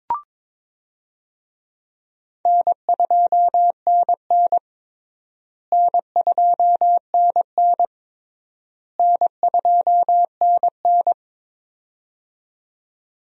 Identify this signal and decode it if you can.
Morse code, heard as a keyed tone, about 22 wpm, 700 Hz. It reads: N2NN N2NN N2NN
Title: USA Call Signs - 4 Characters - Repeated 3x - 22wpm